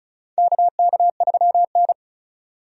Morse code, heard as a keyed tone, about 34 wpm, 700 Hz. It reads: KK3N
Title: USA Call Signs - 4 Characters - 35wpm